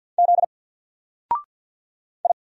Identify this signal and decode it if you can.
Morse code, heard as a keyed tone, about 48 wpm, 700 Hz. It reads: Z I
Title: Single Character (Letter:Number:Punctuation:Prosign) 50wpm